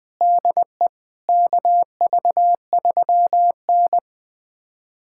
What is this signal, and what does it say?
Morse code, heard as a keyed tone, about 21 wpm, 700 Hz. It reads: DE KV3N
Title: USA Callsign DE Callsign - 4 Characters - 20wpm